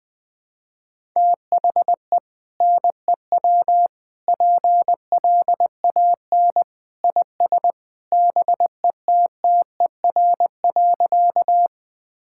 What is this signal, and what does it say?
Morse code, heard as a keyed tone, about 20 wpm, 700 Hz. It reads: THE NEW PLAN IS BETTER.
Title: Sentences with 2-Word Phrases - 20wpm